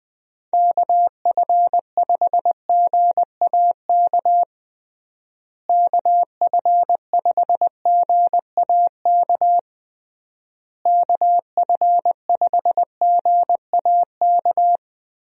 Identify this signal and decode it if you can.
Morse code, heard as a keyed tone, about 21 wpm, 700 Hz. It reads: KF5GAK KF5GAK KF5GAK
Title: USA Call Signs - 6 Characters - Repeated 3x - 20wpm